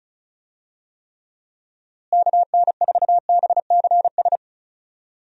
Morse code, heard as a keyed tone, about 36 wpm, 700 Hz. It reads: KN4BCS
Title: USA Call Signs - 6 Characters - Repeated 2x - 35wpm